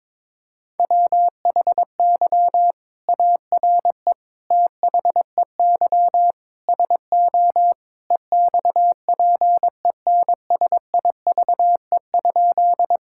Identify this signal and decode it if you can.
Morse code, heard as a keyed tone, about 22 wpm, 700 Hz. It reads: WHY ARE THEY SO EXPENSIVE?
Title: Sentences from Top 1000 Words - Review Part 6 - 22wpm